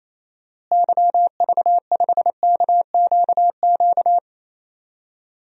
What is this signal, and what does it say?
Morse code, heard as a keyed tone, about 28 wpm, 700 Hz. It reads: YV5KQQ
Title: DXCC-100 Call Signs - 6 Characters - Repeated 3x - 28wpm